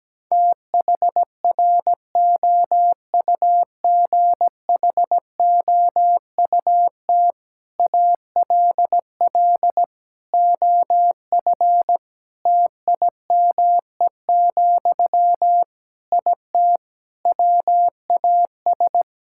Morse code, heard as a keyed tone, about 17 wpm, 700 Hz. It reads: THROUGHOUT ALL OF TIME, IT WAS
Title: Sentences from Top 1000 Words - Review Part 5 - 17wpm